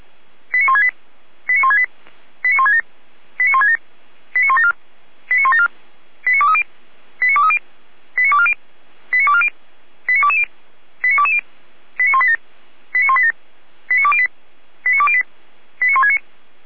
Five-tone selective call (ZVEI), heard as quick runs of five tones, tones of about 70 ms. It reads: 89168 89168 89167 89167 89174 89174 89230 89230 89240 89240 89220 89220 89187 89187 89208 89208 89169